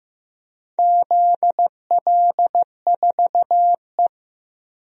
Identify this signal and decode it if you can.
Morse code, heard as a keyed tone, about 15 wpm, 700 Hz. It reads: ZL4E
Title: DXCC-100 Call Signs - 4 Characters - Repeated 2x - 15wpm